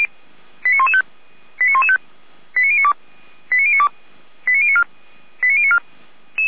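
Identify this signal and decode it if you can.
Five-tone selective call (ZVEI), heard as quick runs of five tones, tones of about 80 ms. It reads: 89115 89115 89092 89092 89094 89094
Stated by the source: ZVEI five-tone selective call 89025, received with an SDR.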